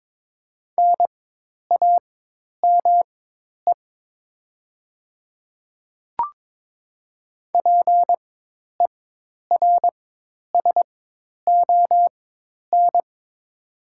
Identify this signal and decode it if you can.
Morse code, heard as a keyed tone, about 22 wpm, spaced as for 10 wpm, 700 Hz. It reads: NAME PERSON
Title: 4x Character Spacing - Top 500 Words - 22wpm